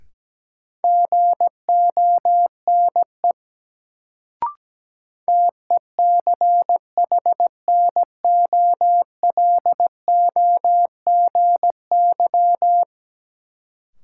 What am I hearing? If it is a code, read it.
Morse code, heard as a keyed tone, about 17 wpm, 700 Hz. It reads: GONE TECHNOLOGY
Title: Sentences from Top 1200 Words - New Word Drill Alt - 17wpm